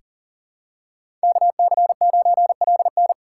Morse code, heard as a keyed tone, about 40 wpm, 700 Hz. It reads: KC9LN
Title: USA Call Signs - 5 Characters - 3x Character Spacing - 40wpm